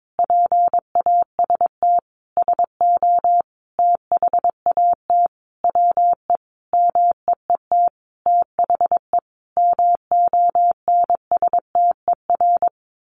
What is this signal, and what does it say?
Morse code, heard as a keyed tone, about 22 wpm, 700 Hz. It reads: PAST SO THAT WE MEET THE MONSTER
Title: A Sound Of Thunder 22wpm